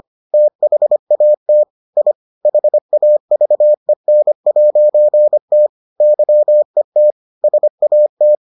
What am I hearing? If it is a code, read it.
Morse code, heard as a keyed tone, about 25 wpm, 600 Hz. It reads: THAT I HAVEN'T YET SAT